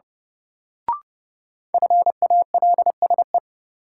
Morse code, heard as a keyed tone, about 31 wpm, 700 Hz. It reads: FALSE
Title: Sentences from Top 1600 Words - New Word Drill - 30wpm